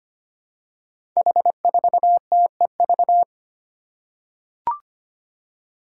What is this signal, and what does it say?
Morse code, heard as a keyed tone, about 25 wpm, 700 Hz. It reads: H4TEV
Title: Five Challenging Letter-Number Combinations 25wpm